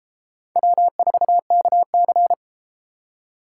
Morse code, heard as a keyed tone, about 33 wpm, 700 Hz. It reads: W4KC